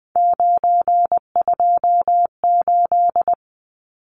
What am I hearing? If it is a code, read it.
Morse code, heard as a keyed tone, about 20 wpm, 700 Hz. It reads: 928
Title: Numbers - 3-Digit 20wpm